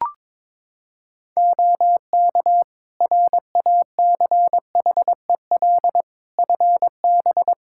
Morse code, heard as a keyed tone, about 22 wpm, 700 Hz. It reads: OK RACHEL FB
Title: K1USN SST QSO Practice 22wpm